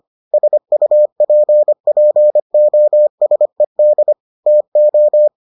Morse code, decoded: SUPPOSED TO

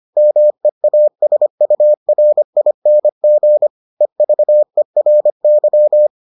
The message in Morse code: MEASURING EVERY

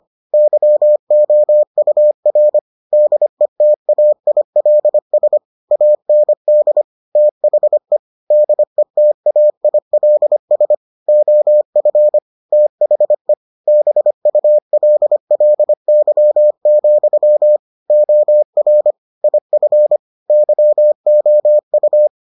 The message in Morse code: YOUR DETAILS AND THE DETAILS OF THE BULLY, OR IF YOU

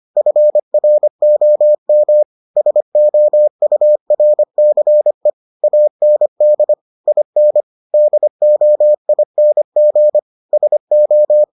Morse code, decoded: FROM SOURCE AND IN DOING SO

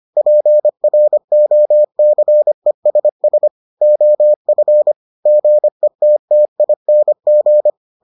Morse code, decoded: PROCESS OF GETTING

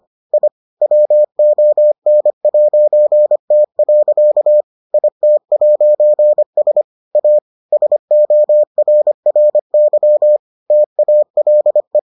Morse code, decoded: I WON'T. IT'S A SORRY TALE

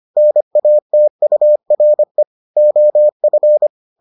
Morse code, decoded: NATURE OF

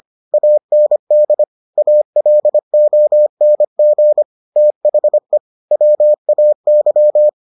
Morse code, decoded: AND ALONG THE WAY

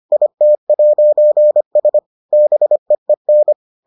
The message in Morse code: IT'S BEEN

